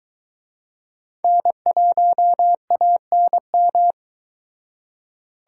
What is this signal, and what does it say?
Morse code, heard as a keyed tone, about 22 wpm, 700 Hz. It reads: N1ANM